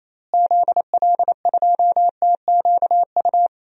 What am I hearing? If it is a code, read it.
Morse code, heard as a keyed tone, about 28 wpm, 700 Hz. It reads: ZL2TQU
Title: DXCC-100 Call Signs - 6 Characters - Repeated 2x - 28wpm